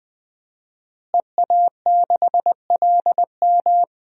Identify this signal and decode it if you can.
Morse code, heard as a keyed tone, about 20 wpm, 700 Hz. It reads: EA6LM
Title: DXCC-100 Call Signs - 5 Characters - Repeated 3x - 20wpm